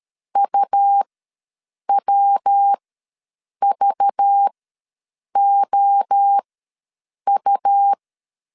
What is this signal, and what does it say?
Morse code, heard as a keyed tone, about 13 wpm, spaced as for 7 wpm, 800 Hz. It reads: UWVOU
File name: LEZIONI / 6 PKX0 / 6 PKX ws 12wpm 3106